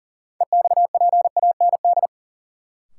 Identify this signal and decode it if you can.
Morse code, heard as a keyed tone, about 40 wpm, 700 Hz. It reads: EXPAND
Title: Sentences from Top 1400 Words - New Word Drill Alt - 40wpm